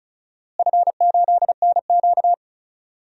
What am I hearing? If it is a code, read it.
Morse code, heard as a keyed tone, about 36 wpm, 700 Hz. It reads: F8NQ
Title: DXCC-100 Call Signs - 4 Characters - Repeated 3x - 35wpm